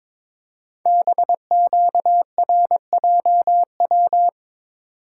Morse code, heard as a keyed tone, about 22 wpm, 700 Hz. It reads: BQRJW
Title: Code Groups - 5 Letters - 22wpm